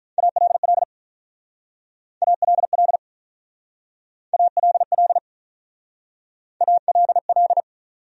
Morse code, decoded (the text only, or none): ALL ALL ALL ALL